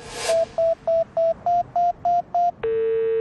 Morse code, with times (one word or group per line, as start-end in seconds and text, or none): none